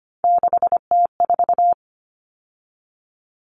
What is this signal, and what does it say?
Morse code, heard as a keyed tone, about 25 wpm, 700 Hz. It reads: 6T4
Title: Common Cut Numbers - 3-Digit 25wpm